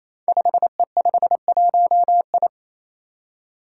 Morse code, heard as a keyed tone, about 28 wpm, 700 Hz. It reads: 5E51I
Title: Five Challenging Letter-Number Combinations 28wpm